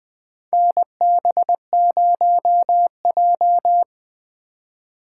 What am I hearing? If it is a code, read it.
Morse code, heard as a keyed tone, about 20 wpm, 700 Hz. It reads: NB0J
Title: USA Call Signs - 3x Character Spacing - 20wpm